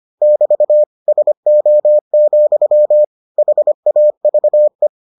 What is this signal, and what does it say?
Morse code, heard as a keyed tone, about 25 wpm, 600 Hz. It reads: <BT> SO, HAVE